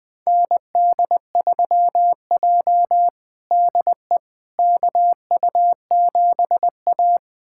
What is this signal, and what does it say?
Morse code, heard as a keyed tone, about 20 wpm, 700 Hz. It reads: ND3J DE KU7A
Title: USA Callsign DE Callsign - 4 Characters - 20wpm